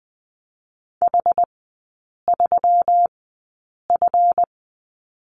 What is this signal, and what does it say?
Morse code, heard as a keyed tone, about 20 wpm, spaced as for 8 wpm, 700 Hz. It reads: H3F
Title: 138 Lesson 23e - Letter f - Three Characters 20wpm